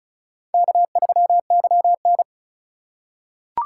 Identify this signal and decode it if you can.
Morse code, heard as a keyed tone, about 36 wpm, 700 Hz. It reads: K3YN
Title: USA Call Signs - 4 Characters - 3x Character Spacing - 35wpm